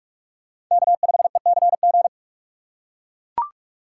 Morse code, heard as a keyed tone, about 44 wpm, 690 Hz. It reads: K5ECG
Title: USA Call Signs - 5 Characters - 8x Character Spacing - 45wpm